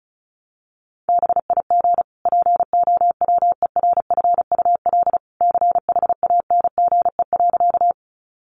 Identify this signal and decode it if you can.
Morse code, heard as a keyed tone, about 34 wpm, 700 Hz. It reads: BIG POWERFUL CHANGE.
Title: Sentences from Top 1300 Words 35wpm